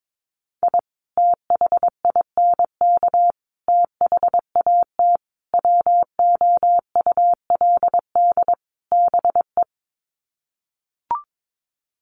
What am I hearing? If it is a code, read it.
Morse code, heard as a keyed tone, about 22 wpm, 700 Hz. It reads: I THINK THAT WOULD BE
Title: Sets of 5 Words Encore2 22wpm